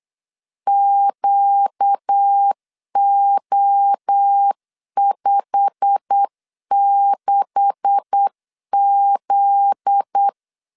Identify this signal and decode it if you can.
Morse code, heard as a keyed tone, about 8 wpm, 800 Hz. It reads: QO56Z
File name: LEZIONI / 9 286 8wpm 3425